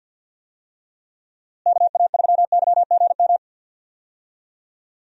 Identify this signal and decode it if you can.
Morse code, heard as a keyed tone, about 48 wpm, 700 Hz. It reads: KA3YGM
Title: USA Call Signs - 6 Characters - Repeated 3x - 50wpm